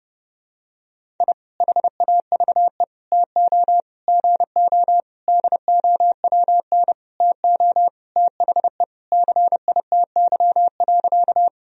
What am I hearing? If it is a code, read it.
Morse code, heard as a keyed tone, about 32 wpm, 700 Hz. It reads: I HAVE TO GO DOWN TO THE CITY.
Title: Sentences from Top 1000 Words - Review Part 1 - 30wpm